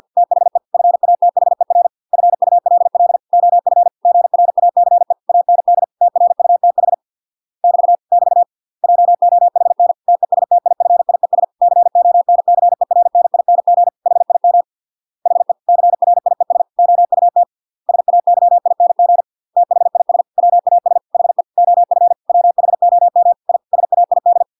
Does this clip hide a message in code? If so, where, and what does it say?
Yes: Morse code from 0.2 to 24.4 s, heard as a keyed tone, about 50 wpm, 700 Hz. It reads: THE FATHER FULL OF GRACE AND TRUTH <BT> <BT> JOHN TESTIFIES CONCERNING HIM HE CRIES OUT SAYING THIS WAS HE OF WHOM I SAID